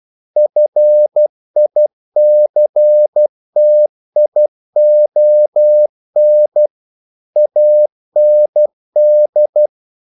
Morse code, heard as a keyed tone, about 12 wpm, 600 Hz. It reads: FICTION AND
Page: English quotations in CW (telegraphy - morse)